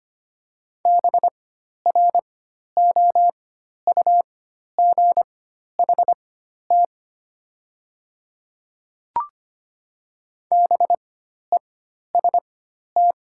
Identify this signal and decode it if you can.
Morse code, heard as a keyed tone, about 25 wpm, spaced as for 12 wpm, 700 Hz. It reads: BROUGHT BEST
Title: 4x Character Spacing - Top 500 Words - 25wpm